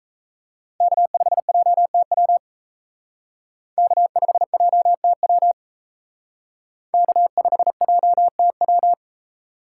Morse code, heard as a keyed tone, about 34 wpm, 700 Hz. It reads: K5JTW K5JTW K5JTW
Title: Speed Racing - USA 1x3 Call Signs 28wpm